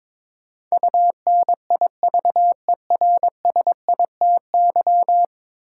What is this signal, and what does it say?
Morse code, heard as a keyed tone, about 22 wpm, 700 Hz. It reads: UNIVERSITY
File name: practice / 22wpm / 10 Letter Words - First 3 Letters Given - 22wpm